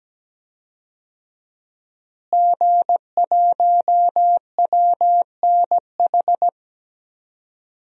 Morse code, heard as a keyed tone, about 17 wpm, 700 Hz. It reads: G1WNH
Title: DXCC-100 Call Signs - 5 Characters - Repeated 3x - 17wpm